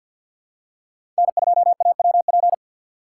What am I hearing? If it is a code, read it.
Morse code, heard as a keyed tone, about 51 wpm, 700 Hz. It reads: N2AWP
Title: USA Call Signs - 5 Characters - 2x Character Spacing - 50wpm